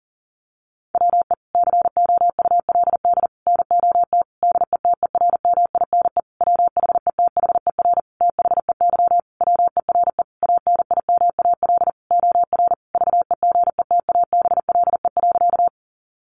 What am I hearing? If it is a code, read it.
Morse code, heard as a keyed tone, about 40 wpm, 700 Hz. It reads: WE COULD NOT DETERMINE WHETHER THEY WERE ANIMAL OR VEGETABLE.
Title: Sentences from Top 1300 Words 40wpm